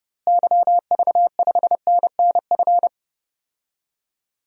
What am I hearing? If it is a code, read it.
Morse code, heard as a keyed tone, about 31 wpm, 700 Hz. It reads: YV5NNF
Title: DXCC-100 Call Signs - 6 Characters - Repeated 3x - 30wpm